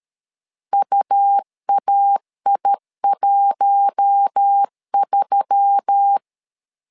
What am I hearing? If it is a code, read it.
Morse code, heard as a keyed tone, about 13 wpm, 800 Hz. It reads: UAI13
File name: LEZIONI / 5 LBVC / 5 L 12wpm 1913